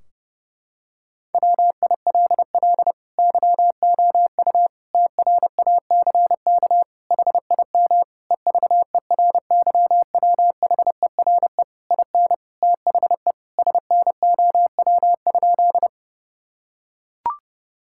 Morse code, heard as a keyed tone, about 28 wpm, 700 Hz. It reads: WILL YOU TRACK HIM EVERYWHERE IN THE SNOW?
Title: Sentences from Top 1600 Words 30wpm